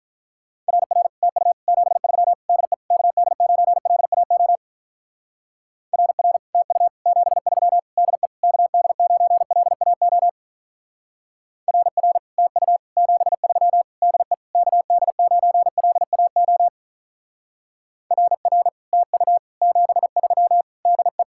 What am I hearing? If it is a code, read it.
Morse code, heard as a keyed tone, about 44 wpm, 700 Hz. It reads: RR TU 73 DE KD9LAO RR TU 73 DE KD9LAO RR TU 73 DE KD9LAO RR TU 73 DE